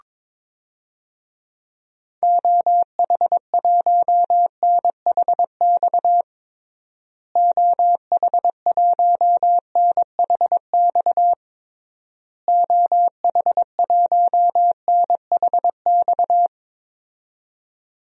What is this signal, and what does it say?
Morse code, heard as a keyed tone, about 22 wpm, 700 Hz. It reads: OH1NHX OH1NHX OH1NHX
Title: DXCC-100 Call Signs - 6 Characters - Repeated 3x - 22wpm